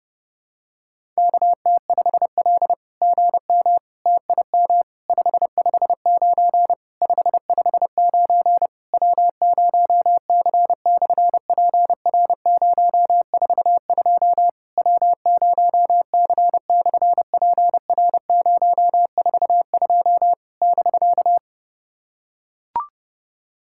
Morse code, heard as a keyed tone, about 30 wpm, 700 Hz. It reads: KT5L GM TIM 559 559 W0C/PR042 W0C/PR042 <BK>
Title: SOTA QSO Practice 30wpm